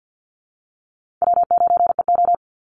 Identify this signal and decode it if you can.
Morse code, heard as a keyed tone, about 50 wpm, 700 Hz. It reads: W9EO
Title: Call Signs Repeated 3x - 50wpm